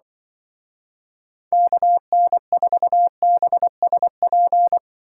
Morse code, heard as a keyed tone, about 25 wpm, 700 Hz. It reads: KN4BSP